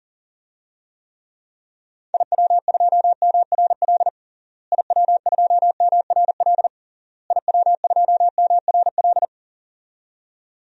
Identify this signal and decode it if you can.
Morse code, heard as a keyed tone, about 40 wpm, 700 Hz. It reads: IW2MRL IW2MRL IW2MRL
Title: DXCC-100 Call Signs - 6 Characters - Repeated 3x - 40wpm